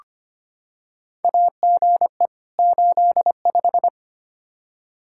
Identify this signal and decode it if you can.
Morse code, heard as a keyed tone, about 25 wpm, 700 Hz. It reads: AGE 85
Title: Practice QSO - Exchange 3 - More Info - 25wpm